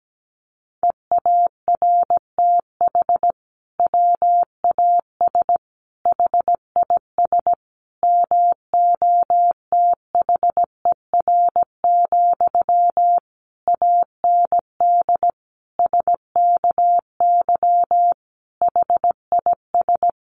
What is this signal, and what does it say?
Morse code, heard as a keyed tone, about 17 wpm, 700 Hz. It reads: EARTH WAS HIS MOTHER, AND SKY HIS
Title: Sentences from Top 1200 Words 17wpm